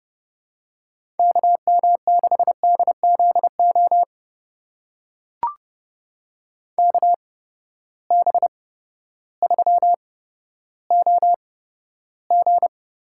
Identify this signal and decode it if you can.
Morse code, heard as a keyed tone, about 30 wpm, spaced as for 14 wpm, 700 Hz. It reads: KM6DZO KB3OG
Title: USA Call Signs - 6 Characters - 8x Character Spacing - 30wpm